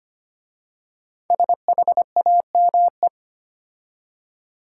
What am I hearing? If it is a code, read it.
Morse code, heard as a keyed tone, about 25 wpm, 700 Hz. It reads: SHAME
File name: 5 Letter Words - First 3 Letters Given - 25wpm